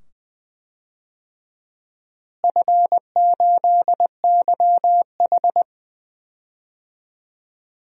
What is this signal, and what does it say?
Morse code, heard as a keyed tone, about 21 wpm, 700 Hz. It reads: F8YH
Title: DXCC-100 Call Signs - 4 Characters - Repeated 2x - 20wpm